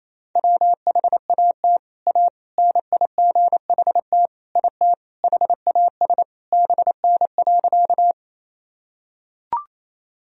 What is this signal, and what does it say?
Morse code, heard as a keyed tone, about 27 wpm, 700 Hz. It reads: WHAT A NIGHT IT HAS BN.
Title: Sentences from Top 300 Words and Abbreviations 28wpm